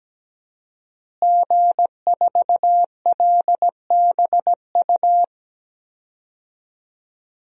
Morse code, heard as a keyed tone, about 17 wpm, 700 Hz. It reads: G4LBU
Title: DXCC-100 Call Signs - 5 Characters - Repeated 3x - 17wpm